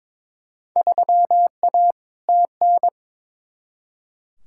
Morse code, heard as a keyed tone, about 23 wpm, 700 Hz. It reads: 3A TN
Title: Field Day Exchanges - 22wpm